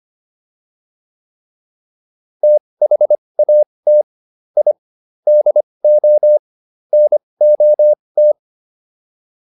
Morse code, heard as a keyed tone, about 25 wpm, spaced as for 20 wpm, 600 Hz. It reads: THAT I DO NOT